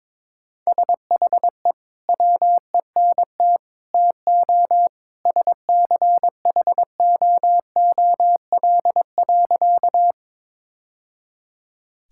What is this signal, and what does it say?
Morse code, heard as a keyed tone, about 22 wpm, 700 Hz. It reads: SHE WENT TO SCHOOL.
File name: practice / 22wpm / Sentences with 2-Word Phrases - 22wpm